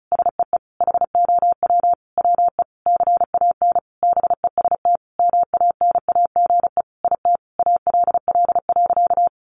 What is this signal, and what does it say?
Morse code, heard as a keyed tone, about 34 wpm, 700 Hz. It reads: SEE HOW WE CAN BEST MANAGE IT ALL.
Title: Sentences from Top 1400 Words 35wpm